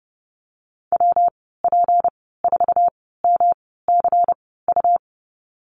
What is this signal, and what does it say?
Morse code, heard as a keyed tone, about 32 wpm, spaced as for 17 wpm, 700 Hz. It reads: WP4MCU
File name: Lesson 37g - Letter x - Call Signs 30wpm